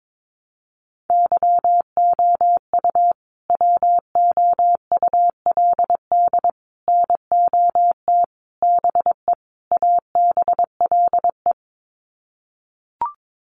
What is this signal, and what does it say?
Morse code, heard as a keyed tone, about 22 wpm, 700 Hz. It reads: YOU WOULD NOT BE ABLE
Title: Sets of 5 Words Encore2 22wpm